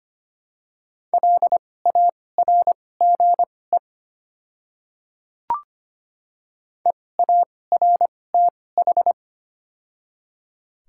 Morse code, heard as a keyed tone, about 25 wpm, spaced as for 18 wpm, 700 Hz. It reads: LARGE EARTH
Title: 2x Character Spacing - Top 300 Words - 25wpm